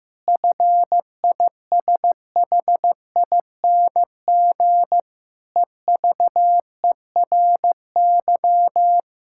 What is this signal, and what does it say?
Morse code, heard as a keyed tone, about 16 wpm, 700 Hz. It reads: FISHING EVERY
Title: Sentences from Top 1000 Words - Review Part 5 - 15wpm